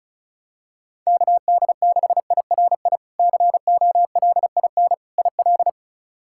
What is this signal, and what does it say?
Morse code, heard as a keyed tone, about 34 wpm, 700 Hz. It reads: KD6IRI COLIN IL
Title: K1USN SST QSO Practice - 35wpm